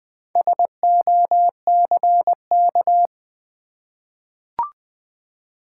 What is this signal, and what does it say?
Morse code, heard as a keyed tone, about 20 wpm, 700 Hz. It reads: SOCK
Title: Four Letter Words - 20wpm